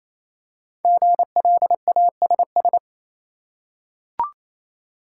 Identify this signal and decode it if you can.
Morse code, heard as a keyed tone, about 28 wpm, 700 Hz. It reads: GLASS
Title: Five Letter Words - 28wpm